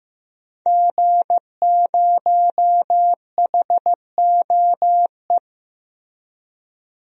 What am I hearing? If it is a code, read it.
Morse code, heard as a keyed tone, about 15 wpm, 700 Hz. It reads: G0HOE
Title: DXCC-100 Call Signs - 5 Characters - Repeated 3x - 15wpm